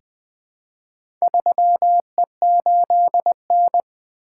Morse code, heard as a keyed tone, about 20 wpm, 700 Hz. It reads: 3E8N
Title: Four Letter-Number Combinations - 20wpm